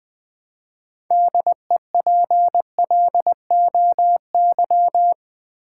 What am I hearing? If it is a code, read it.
Morse code, heard as a keyed tone, about 20 wpm, 700 Hz. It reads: DEPLOY
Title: Build A Word - 2 Letters At A Time - 6-Letter Words - 20wpm